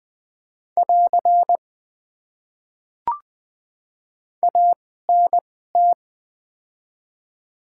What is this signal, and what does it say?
Morse code, heard as a keyed tone, about 20 wpm, spaced as for 15 wpm, 700 Hz. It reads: <AR> ANT
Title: Top 90 QSO Elements - 2x Character Spacing - 20wpm